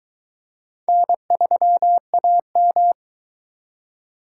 Morse code, heard as a keyed tone, about 22 wpm, 700 Hz. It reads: N3AM